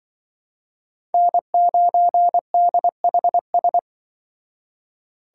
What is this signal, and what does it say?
Morse code, heard as a keyed tone, about 24 wpm, 700 Hz. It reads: N9DHS
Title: Speed Racing - USA Call Signs 20wpm